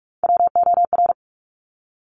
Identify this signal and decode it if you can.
Morse code, heard as a keyed tone, about 46 wpm, 700 Hz. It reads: WOR
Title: Top Three Letter Combinations in Words 45wpm